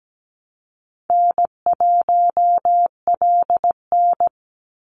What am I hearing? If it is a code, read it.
Morse code, heard as a keyed tone, about 17 wpm, 700 Hz. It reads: N1LN
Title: Call Signs Repeated 3x - 17wpm